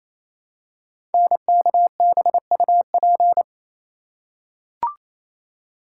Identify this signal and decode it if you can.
Morse code, heard as a keyed tone, about 29 wpm, 700 Hz. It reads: NKBUP
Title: Code Groups - 5 Letters - 28wpm